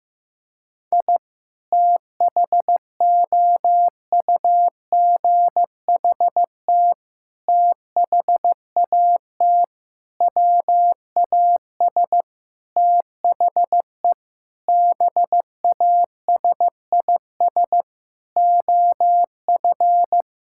Morse code, heard as a keyed tone, about 15 wpm, 700 Hz. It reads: I THOUGHT THAT WAS THE BASIS OF
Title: Sentences from Top 1000 Words - Review Part 5 - 15wpm